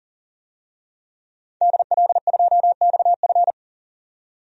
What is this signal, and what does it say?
Morse code, heard as a keyed tone, about 40 wpm, 700 Hz. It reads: DL2XF
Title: DXCC-100 Call Signs - 5 Characters - 40wpm